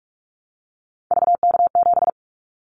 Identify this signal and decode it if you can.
Morse code, heard as a keyed tone, about 46 wpm, 700 Hz. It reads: VK7
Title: Callsign Prefixes for Top 100 DXCC - 45wpm